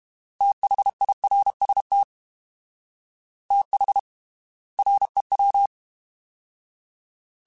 Morse code, heard as a keyed tone, about 32 wpm, spaced as for 25 wpm, 800 Hz. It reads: THIRST THREW
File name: TH / TH mp3s / Fast TH @30